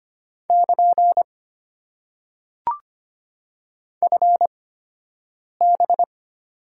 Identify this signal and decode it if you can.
Morse code, heard as a keyed tone, about 25 wpm, spaced as for 7 wpm, 690 Hz. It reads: <KN> FB
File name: Top 90 QSO Elements - 8x Character Spacing - 25wpm